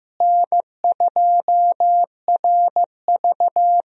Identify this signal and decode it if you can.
Morse code, heard as a keyed tone, about 15 wpm, 700 Hz. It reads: N2RV
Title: USA Call Signs - 4 Characters - Repeated 3x - 15wpm